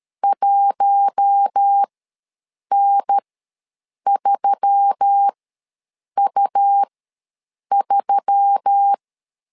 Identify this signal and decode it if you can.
Morse code, heard as a keyed tone, about 13 wpm, spaced as for 7 wpm, 800 Hz. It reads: 1N3U3
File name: LEZIONI / 4 Q1S3 ws 12wpm 2038